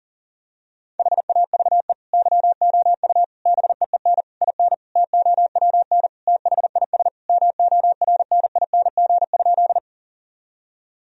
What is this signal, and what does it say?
Morse code, heard as a keyed tone, about 40 wpm, 700 Hz. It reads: HAVE YOU BEEN IN TOWN THIS MORNING?
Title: Sentences from Top 1000 Words - Review Part 2 - 40wpm